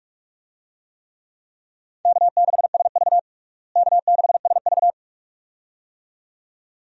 Morse code, heard as a keyed tone, about 48 wpm, 700 Hz. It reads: K6SV K6SV
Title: USA Call Signs - 4 Characters - Repeated 2x - 45wpm